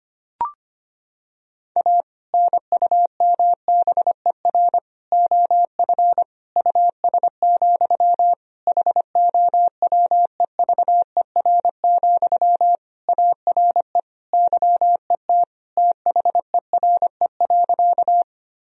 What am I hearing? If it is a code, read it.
Morse code, heard as a keyed tone, about 25 wpm, 700 Hz. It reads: A NUMBER OF US, HOWEVER, ARE YET THERE.
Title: Sentences from Top 1000 Words - Review Part 1 - 25wpm